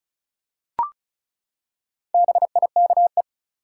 Morse code, heard as a keyed tone, about 36 wpm, 700 Hz. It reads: BIKE
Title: Four Letter Words - 35wpm